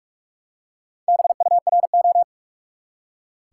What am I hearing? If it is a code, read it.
Morse code, heard as a keyed tone, about 46 wpm, 700 Hz. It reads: BURO
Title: Top 90 QSO Elements 45wpm